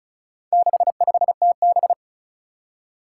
Morse code, heard as a keyed tone, about 36 wpm, 700 Hz. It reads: 65TB
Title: Four Challenging Letter-Number Combinations 35wpm